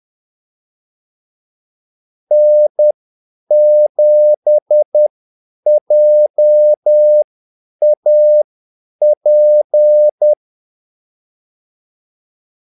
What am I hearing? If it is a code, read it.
Morse code, heard as a keyed tone, about 10 wpm, spaced as for 8 wpm, 600 Hz. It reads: N7JAP
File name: Callsigns-10-8wpm